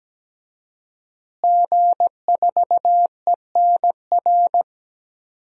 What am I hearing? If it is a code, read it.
Morse code, heard as a keyed tone, about 17 wpm, 700 Hz. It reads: G4ENR
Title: DXCC-100 Call Signs - 5 Characters - Repeated 3x - 17wpm